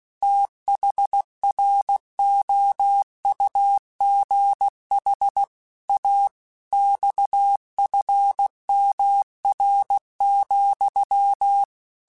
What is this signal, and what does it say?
Morse code, heard as a keyed tone, about 16 wpm, 800 Hz. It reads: THROUGH A XFMR,
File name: Technical Subjects / Technical Subjects-mp3s / Sw Pwr Sup @15